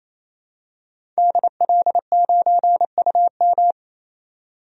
Morse code, heard as a keyed tone, about 28 wpm, 700 Hz. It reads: DL9UM
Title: DXCC-100 Call Signs - 5 Characters - 28wpm